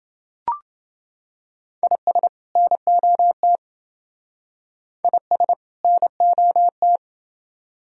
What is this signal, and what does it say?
Morse code, heard as a keyed tone, about 27 wpm, 700 Hz. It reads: IS NOT IS NOT